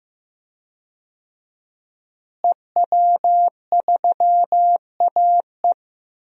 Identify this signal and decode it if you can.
Morse code, heard as a keyed tone, about 15 wpm, 700 Hz. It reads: EW3AE
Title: DXCC-100 Call Signs - 5 Characters - Repeated 2x - 15wpm